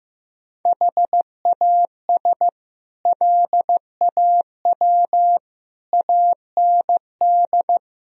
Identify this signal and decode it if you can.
Morse code, heard as a keyed tone, about 16 wpm, 700 Hz. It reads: HAS LAW AND
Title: Sentences from Top 1000 Words - Review Part 3 - 15wpm